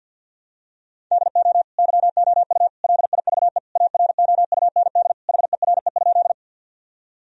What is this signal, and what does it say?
Morse code, heard as a keyed tone, about 52 wpm, 700 Hz. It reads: DO YOU LIVE AROUND HERE?
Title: Sentences from Top 1000 Words - Review Part 3 - 50wpm